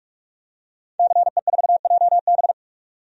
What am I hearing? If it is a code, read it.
Morse code, heard as a keyed tone, about 44 wpm, 690 Hz. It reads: CE4JB
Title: DXCC-100 Call Signs - 5 Characters - 45wpm